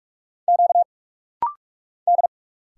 Morse code, heard as a keyed tone, about 46 wpm, 700 Hz. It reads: Q D
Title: Single Letter-Number - Rapid-Fire 45wpm